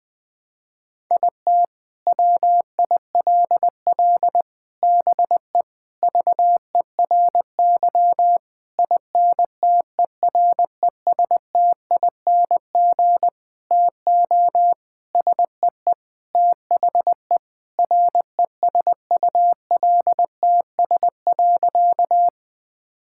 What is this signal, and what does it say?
Morse code, heard as a keyed tone, about 20 wpm, 700 Hz. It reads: IT WILL BE VERY INTERESTING TO SEE THE RESULTS.
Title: Sentences with 7-Word Phrases - 20wpm